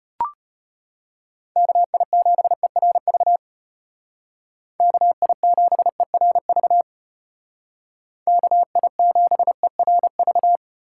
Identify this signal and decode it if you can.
Morse code, heard as a keyed tone, about 33 wpm, 700 Hz. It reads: KI7ERV KI7ERV KI7ERV